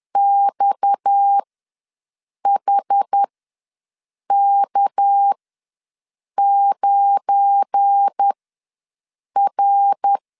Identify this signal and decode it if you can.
Morse code, heard as a keyed tone, about 10 wpm, spaced as for 6 wpm, 800 Hz. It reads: XHK9R
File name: LEZIONI / 9 2 ws 10wpm 4130